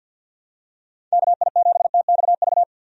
Morse code, heard as a keyed tone, about 52 wpm, 700 Hz. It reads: KI7TXV
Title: USA Call Signs - 6 Characters - 50wpm